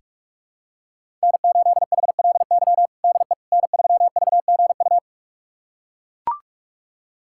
Morse code, heard as a keyed tone, about 42 wpm, 700 Hz. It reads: N8HLY DE N3VGU
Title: USA Callsign DE Callsign - 5 Characters - 45wpm